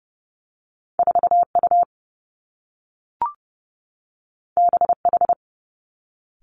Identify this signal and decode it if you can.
Morse code, heard as a keyed tone, about 30 wpm, 700 Hz. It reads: 4 U B H
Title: Two Challenging Letter-Number Combinations 30wpm